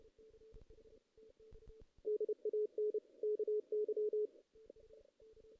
Morse code, heard as a keyed tone, about 29 wpm, 420 Hz. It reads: DAN KY
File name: 2016 JAN CW NAQP / 2016 Jan naqp k5ot - 0-7